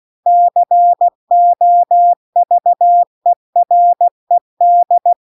Morse code, heard as a keyed tone, about 16 wpm, 700 Hz. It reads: COVERED
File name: CW-The-Big-Twoheaded-River-2-16WPM